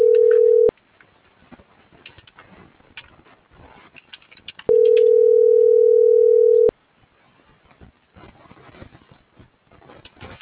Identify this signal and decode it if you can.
Ringback tone (North America), 440 + 480 Hz.